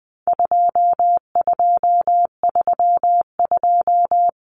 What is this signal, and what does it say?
Morse code, heard as a keyed tone, about 20 wpm, 700 Hz. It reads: 2232
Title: Common Cut Numbers - 4-Digit 20wpm